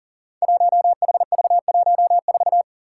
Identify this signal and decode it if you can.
Morse code, heard as a keyed tone, about 39 wpm, 700 Hz. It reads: 1HV14
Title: Five Challenging Letter-Number Combinations 40wpm